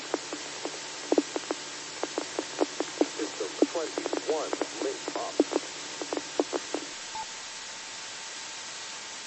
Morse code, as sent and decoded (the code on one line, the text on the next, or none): none